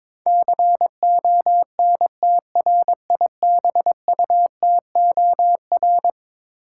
Morse code, heard as a keyed tone, about 22 wpm, 700 Hz. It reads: CONTRIBUTOR
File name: practice / 22wpm / Eleven Letter Words - 22wpm